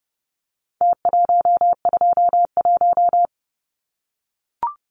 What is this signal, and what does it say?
Morse code, heard as a keyed tone, about 30 wpm, 700 Hz. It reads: T121
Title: Numbers - Contest Style - 4-Digit 30wpm